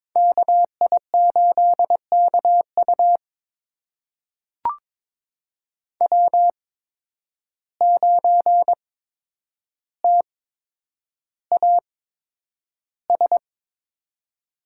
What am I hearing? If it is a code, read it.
Morse code, heard as a keyed tone, about 22 wpm, spaced as for 9 wpm, 700 Hz. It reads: KI8KU W9TAS
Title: USA Call Signs - 5 Characters - 8x Character Spacing - 22wpm